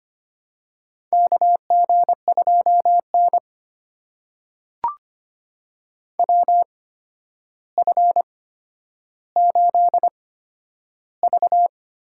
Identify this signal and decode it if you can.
Morse code, heard as a keyed tone, about 25 wpm, spaced as for 11 wpm, 700 Hz. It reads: KG2N WF8V
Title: USA Call Signs - 4 Characters - 8x Character Spacing - 25wpm